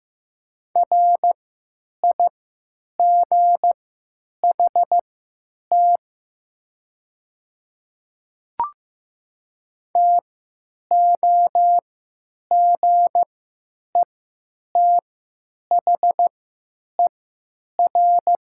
Morse code, heard as a keyed tone, about 15 wpm, spaced as for 9 wpm, 700 Hz. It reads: RIGHT TOGETHER
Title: 3x Character Spacing - Top 300 Words - 15wpm